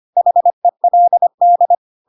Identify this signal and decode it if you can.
Morse code, heard as a keyed tone, about 25 wpm, 700 Hz. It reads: HELD
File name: Pinocchio / Pinocchio ch3 25 25 ik1hge